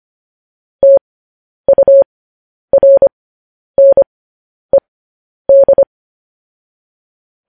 Morse code, heard as a keyed tone, about 25 wpm, spaced as for 10 wpm, 550 Hz. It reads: TURNED